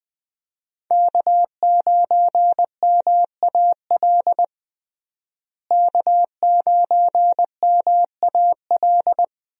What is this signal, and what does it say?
Morse code, heard as a keyed tone, about 20 wpm, 700 Hz. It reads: K9MAL K9MAL
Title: USA Call Signs - 5 Characters - Repeated 3x - 20wpm